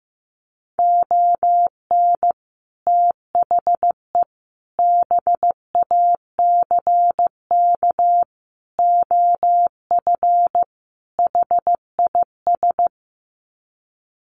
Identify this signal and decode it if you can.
Morse code, heard as a keyed tone, about 15 wpm, 700 Hz. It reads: ON THE BACK OF HIS
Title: Sets of 5 Words Encore 15wpm